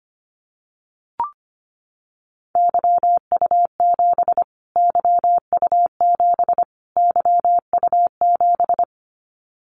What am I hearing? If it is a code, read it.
Morse code, heard as a keyed tone, about 25 wpm, 700 Hz. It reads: YU7 YU7 YU7
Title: Callsign Prefixes for Top 25 DXCC - Repeated 3x - 25wpm